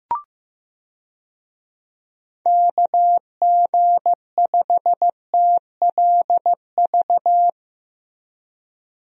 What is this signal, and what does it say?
Morse code, heard as a keyed tone, about 15 wpm, 700 Hz. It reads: KG5TLV
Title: USA Call Signs - 6 Characters - Repeated 3x - 15wpm